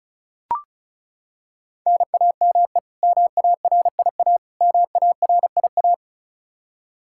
Morse code, heard as a keyed tone, about 34 wpm, 700 Hz. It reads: NAME MARIA MARIA
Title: Practice QSO - Exchange 1 - Beginning - 35wpm